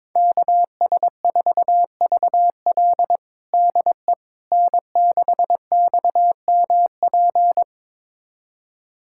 Morse code, heard as a keyed tone, about 22 wpm, 700 Hz. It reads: KS4VL DE N6XMP
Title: USA Callsign DE Callsign - 5 Characters - 22wpm